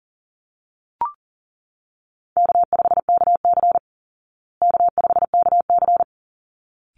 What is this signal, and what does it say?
Morse code, heard as a keyed tone, about 40 wpm, 700 Hz. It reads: K5KC K5KC
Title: Easy Call Signs Repeated 2x - 40wpm